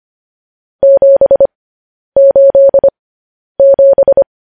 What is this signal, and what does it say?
Morse code, heard as a keyed tone, about 25 wpm, spaced as for 10 wpm, 550 Hz. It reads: 787
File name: podcast / News headlines-1x-25x10